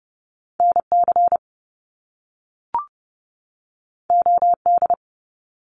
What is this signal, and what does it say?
Morse code, heard as a keyed tone, about 30 wpm, 700 Hz. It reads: NC OD
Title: Top Two Letter Combinations in Words 30wpm